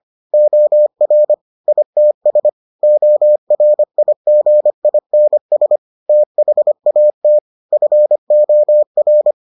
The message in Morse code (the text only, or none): OR ITS ORIGINS THAT FOR